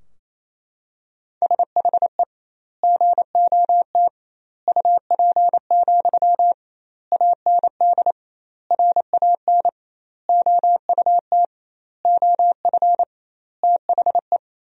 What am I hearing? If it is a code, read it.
Morse code, heard as a keyed tone, about 28 wpm, 700 Hz. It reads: SHE GOT UP, AND RAN OUT OF THE
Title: 2x Word-Spacing - Sentences from Top 500 Words - 28wpm